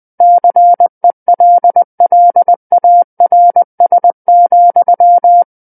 Morse code, heard as a keyed tone, about 20 wpm, 700 Hz. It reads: CELLARS,
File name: 20 WPM